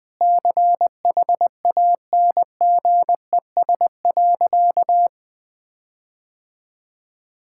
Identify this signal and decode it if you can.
Morse code, heard as a keyed tone, about 20 wpm, 700 Hz. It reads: CHANGES.
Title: Sentences with 6-Word Phrases - 20wpm